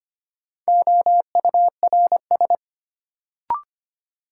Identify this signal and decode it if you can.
Morse code, heard as a keyed tone, about 25 wpm, 700 Hz. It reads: OURS
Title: Sentences from Top 1300 Words - New Word Drill Alt - 25wpm